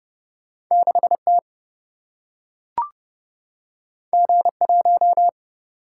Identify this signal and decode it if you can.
Morse code, heard as a keyed tone, about 31 wpm, 700 Hz. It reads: 6T G1
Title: Single-Letter Single-Number 30wpm